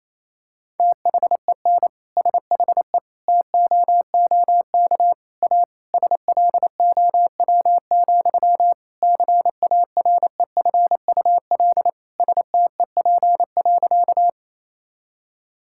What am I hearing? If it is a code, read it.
Morse code, heard as a keyed tone, about 28 wpm, 700 Hz. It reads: THEN SHE TOOK A SLOW, CAREFUL STEP.
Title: Sentences from Top 1600 Words 28wpm